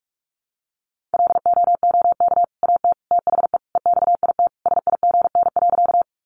Morse code, heard as a keyed tone, about 44 wpm, 700 Hz. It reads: LOOK AT THE EXIT SIGN.
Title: Sentences from Top 1100 Words 45wpm